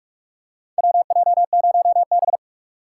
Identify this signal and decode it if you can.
Morse code, heard as a keyed tone, about 46 wpm, 700 Hz. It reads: WJ0B
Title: USA Call Signs - 6x Character Spacing - 45wpm